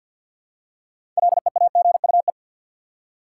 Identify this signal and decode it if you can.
Morse code, heard as a keyed tone, about 53 wpm, 700 Hz. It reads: LEAGUE
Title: Six Letter Words - 50wpm